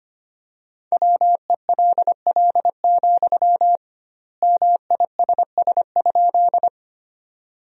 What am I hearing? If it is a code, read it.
Morse code, heard as a keyed tone, about 25 wpm, 700 Hz. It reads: WELL, MISS?
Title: 2x Word-Spacing - Sentences from Top 500 Words - 25wpm